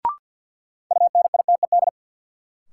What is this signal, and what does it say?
Morse code, heard as a keyed tone, about 50 wpm, 700 Hz. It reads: UNITED
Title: Sentences from Top 600 Words - New Word Drill - 50wpm